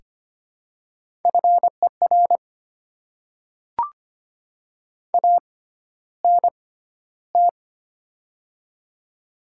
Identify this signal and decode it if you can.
Morse code, heard as a keyed tone, about 25 wpm, spaced as for 13 wpm, 690 Hz. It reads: FER ANT
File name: Top 20 QSO Elements - 6x Character Spacing - 25wpm